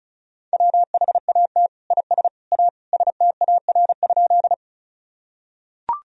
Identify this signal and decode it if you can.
Morse code, heard as a keyed tone, about 36 wpm, 700 Hz. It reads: WHAT IS A STAR?
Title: Sentences from Top 1000 Words - Review Part 2 - 35wpm